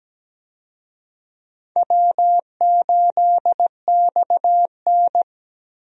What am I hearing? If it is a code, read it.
Morse code, heard as a keyed tone, about 17 wpm, 700 Hz. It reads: W8XN
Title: USA Call Signs - 4 Characters - 17wpm